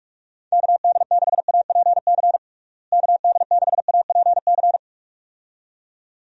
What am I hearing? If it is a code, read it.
Morse code, heard as a keyed tone, about 41 wpm, 700 Hz. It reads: KD6APC KD6APC
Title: USA Call Signs - 6 Characters - Repeated 2x - 45wpm